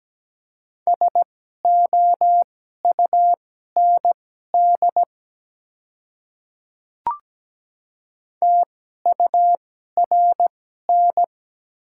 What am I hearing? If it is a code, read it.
Morse code, heard as a keyed tone, about 17 wpm, spaced as for 12 wpm, 700 Hz. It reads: SOUND TURN
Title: 2x Character Spacing - Top 200 Words - 17wpm